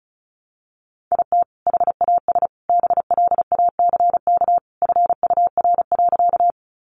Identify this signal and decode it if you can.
Morse code, heard as a keyed tone, about 34 wpm, 700 Hz. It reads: IT HAS BLACK FUR.
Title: Easy Phrases 35wpm